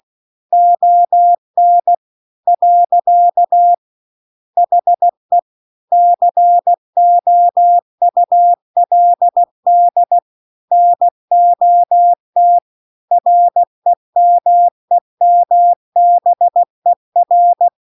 Morse code, heard as a keyed tone, about 16 wpm, 700 Hz. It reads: ON . HE COULD NOT REMEMBER